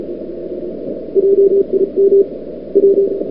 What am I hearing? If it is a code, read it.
Morse code, heard as a keyed tone, about 34 wpm, 440 Hz. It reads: JIM W
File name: CWSprint / February 2007 CW Sprint / 01-2007 CW Sprint Hour1